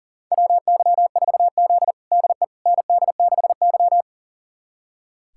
Morse code, heard as a keyed tone, about 40 wpm, 700 Hz. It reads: WY4Z DE ND6Y
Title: USA Callsign DE Callsign - 4 Characters - 40wpm